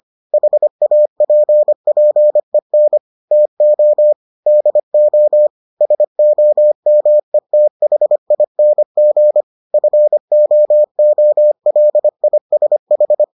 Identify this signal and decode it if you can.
Morse code, heard as a keyed tone, about 25 wpm, 600 Hz. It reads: HAPPEN TO DO SOMETHING FOOLISH